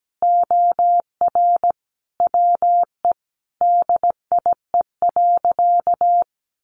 Morse code, heard as a keyed tone, about 17 wpm, 700 Hz. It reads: OR WE DIE.
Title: Sentences from Top 1300 Words 17wpm